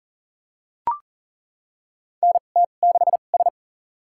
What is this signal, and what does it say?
Morse code, heard as a keyed tone, about 37 wpm, spaced as for 28 wpm, 690 Hz. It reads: NT6S
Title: USA Call Signs - 4 Characters - 2x Character Spacing - 40wpm